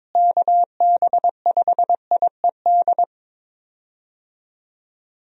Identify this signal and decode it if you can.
Morse code, heard as a keyed tone, about 22 wpm, 700 Hz. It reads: KB5IED
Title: USA Call Signs - 6 Characters - Repeated 2x - 22wpm